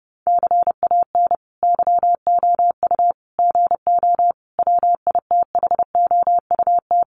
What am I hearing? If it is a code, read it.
Morse code, heard as a keyed tone, about 30 wpm, 700 Hz. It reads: CAN YOU GO WITHOUT
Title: Sentences from Top 1500 Words 30wpm